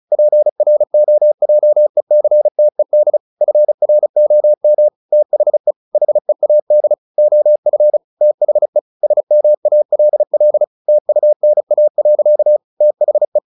Morse code, heard as a keyed tone, about 34 wpm, 600 Hz. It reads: PROJECTED FROM THE HEAD OF THE SMALL TUNA. THE
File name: the old man and the sea / the old man and the sea 1 35wpm